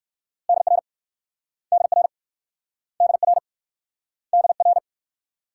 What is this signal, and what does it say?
Morse code, heard as a keyed tone, about 60 wpm, 700 Hz. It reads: DR DR DR DR